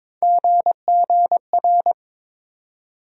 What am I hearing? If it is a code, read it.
Morse code, heard as a keyed tone, about 21 wpm, 700 Hz. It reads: GGR
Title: USA Call Sign Postfixes - 3 Letters 22wpm